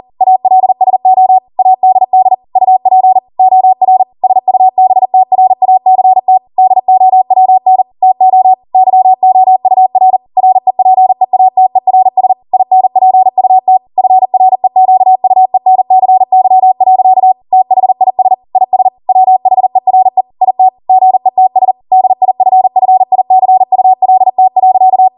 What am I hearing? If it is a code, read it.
Morse code, heard as a keyed tone, about 40 wpm, 750 Hz. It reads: ALSO ADD UP OR SUBTRACT DOWN TO YOUR REPEATERS INPUT FREQUENCY. THIS IS WHERE IT GETS DIFFICULT.
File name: Morse / 40WPM